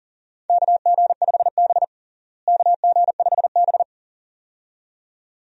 Morse code, heard as a keyed tone, about 40 wpm, 700 Hz. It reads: KG5B KG5B
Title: USA Call Signs - 4 Characters - Repeated 2x - 40wpm